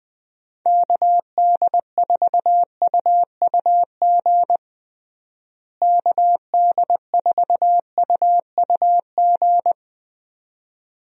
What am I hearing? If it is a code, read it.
Morse code, heard as a keyed tone, about 20 wpm, 700 Hz. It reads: KD4UUG KD4UUG
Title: USA Call Signs - 6 Characters - Repeated 2x - 20wpm